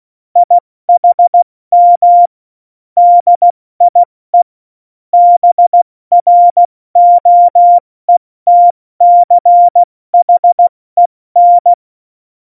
Morse code, heard as a keyed tone, about 16 wpm, 700 Hz. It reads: IHM DIE BROETCHEN
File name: Morsekurs / Witze / Witz 11. September 2018, 70 F80